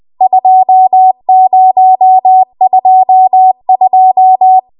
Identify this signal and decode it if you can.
Morse code, heard as a keyed tone, about 20 wpm, 750 Hz. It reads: 2022
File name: Archive / 20 WPM / 20WPM